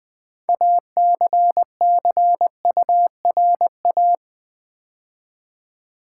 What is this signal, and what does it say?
Morse code, heard as a keyed tone, about 21 wpm, 700 Hz. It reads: ACCURA
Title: Build A Word - 3 Letters At A Time - 8-Letter Words - 20wpm